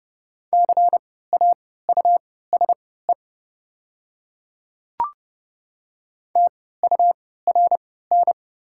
Morse code, heard as a keyed tone, about 30 wpm, spaced as for 17 wpm, 700 Hz. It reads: CAUSE TURN
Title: 3x Character Spacing - Top 300 Words - 30wpm